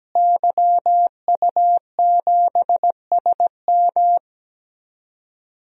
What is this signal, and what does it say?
Morse code, heard as a keyed tone, about 17 wpm, 700 Hz. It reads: YU7SM
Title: DXCC-100 Call Signs - 5 Characters - Repeated 3x - 17wpm